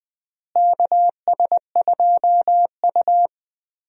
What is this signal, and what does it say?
Morse code, heard as a keyed tone, about 20 wpm, 700 Hz. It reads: KS2U
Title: USA Call Signs - 4 Characters - Repeated 2x - 20wpm